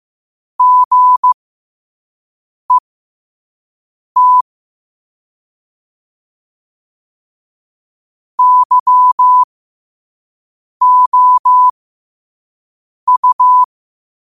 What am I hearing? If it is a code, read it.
Morse code, heard as a keyed tone, about 15 wpm, spaced as for 5 wpm, 1000 Hz. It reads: GET YOU
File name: qotdfiles / 5wpm